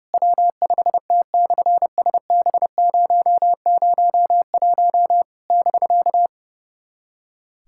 Morse code, heard as a keyed tone, about 30 wpm, 700 Hz. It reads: W5T/SB001 <BK>
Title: SOTA QSO Practice 30wpm